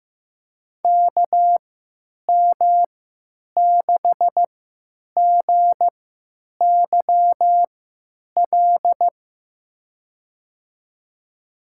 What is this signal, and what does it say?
Morse code, heard as a keyed tone, about 15 wpm, spaced as for 9 wpm, 700 Hz. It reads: KM6GYL
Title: USA Call Signs - 6 Characters - 3x Character Spacing - 15wpm